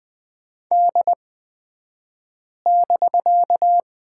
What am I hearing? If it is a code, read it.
Morse code, heard as a keyed tone, about 20 wpm, spaced as for 5 wpm, 700 Hz. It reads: D<BK>
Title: 249 Lesson 40d - BK - Two Characters 20wpm